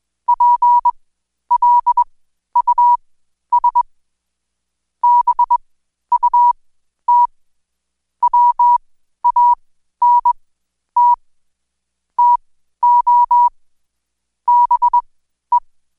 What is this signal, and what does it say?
Morse code, heard as a keyed tone, about 20 wpm, spaced as for 11 wpm, 970 Hz. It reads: PLUS BUT WANT TO BE
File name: Code Quick / AudioDownld2 / Seg121-Day26-DayDrill 26